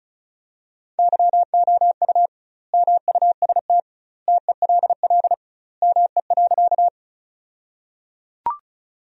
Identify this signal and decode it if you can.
Morse code, heard as a keyed tone, about 36 wpm, 700 Hz. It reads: YOU MUST TELL ME.
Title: 2x Word-Spacing - Sentences from Top 200 Words - 35wpm